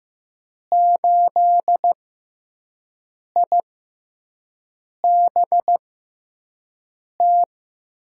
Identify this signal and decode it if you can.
Morse code, heard as a keyed tone, about 15 wpm, spaced as for 5 wpm, 700 Hz. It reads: 8IBT
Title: USA Call Signs - 6 Characters - 6x Character Spacing - 15wpm